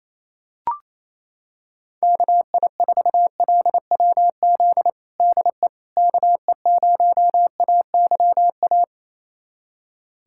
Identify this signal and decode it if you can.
Morse code, heard as a keyed tone, about 28 wpm, 700 Hz. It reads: KI4LWZ DE KE0AYA
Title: USA Callsign DE Callsign - 6 Characters - 28wpm